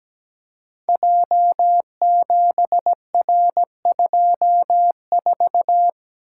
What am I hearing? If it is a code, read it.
Morse code, heard as a keyed tone, about 17 wpm, 700 Hz. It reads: J7R24
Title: Code Groups - 5 Letter:Numbers - 17wpm